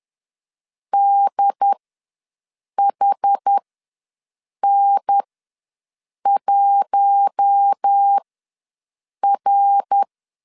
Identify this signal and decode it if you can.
Morse code, heard as a keyed tone, about 10 wpm, spaced as for 6 wpm, 800 Hz. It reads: DHN1R
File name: LEZIONI / 5 LBVC / 5 L ws 10wpm 3636